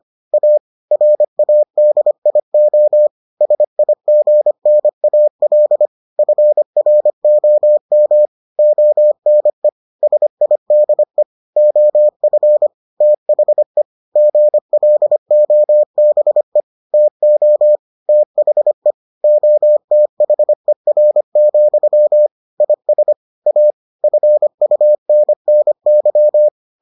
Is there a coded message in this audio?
Yes: Morse code from 0.3 to 26.5 s, heard as a keyed tone, about 25 wpm, 600 Hz. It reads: A RADIO SIGNAL FROM ONE SIDE OF THE GLOBE TO THE OTHER, IS A FUNNY